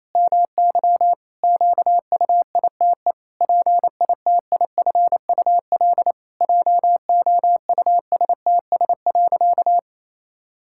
Morse code, heard as a keyed tone, about 28 wpm, 700 Hz. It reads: MY QUITE PITIFUL JOUSTS.
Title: Pangram Challenge v2 - 28wpm